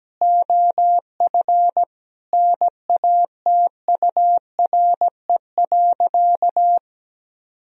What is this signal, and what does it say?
Morse code, heard as a keyed tone, about 17 wpm, 700 Hz. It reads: OF NATURE.
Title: Sentences from Top 1000 Words - Review Part 3 - 17wpm